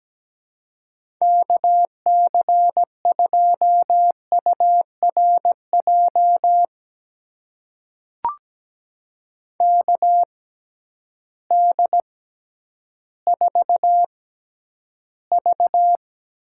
Morse code, heard as a keyed tone, about 17 wpm, spaced as for 10 wpm, 700 Hz. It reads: KC2URJ KD4V
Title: USA Call Signs - 6 Characters - 6x Character Spacing - 17wpm